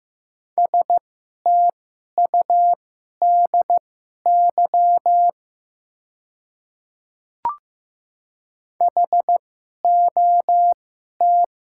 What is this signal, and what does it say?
Morse code, heard as a keyed tone, about 15 wpm, spaced as for 11 wpm, 700 Hz. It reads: STUDY HOT
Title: 2x Character Spacing - Top 300 Words - 15wpm